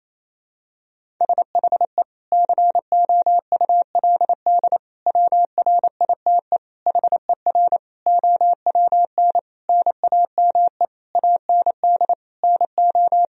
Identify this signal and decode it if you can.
Morse code, heard as a keyed tone, about 28 wpm, 700 Hz. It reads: SHE COULD WRITE HER OWN NAME AND NO
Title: Sentences from Top 1000 Words - Review Part 1 - 28wpm